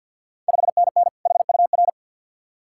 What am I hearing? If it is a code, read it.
Morse code, heard as a keyed tone, about 52 wpm, 690 Hz. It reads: 5NN HUR
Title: Ontario QSO Party Exchange 50wpm